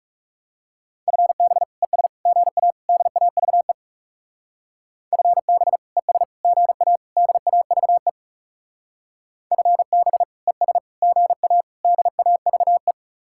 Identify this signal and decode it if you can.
Morse code, heard as a keyed tone, about 37 wpm, 700 Hz. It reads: FB ES GA DAVE FB ES GA DAVE FB ES GA DAVE
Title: Speed-Racing Common QSO Phrases Encore - 30wpm